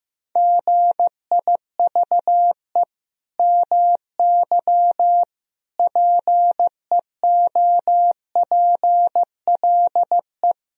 Morse code, heard as a keyed tone, about 16 wpm, 700 Hz. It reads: GIVE MY PEOPLE
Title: Sentences from Top 1000 Words - Review Part 1 - 15wpm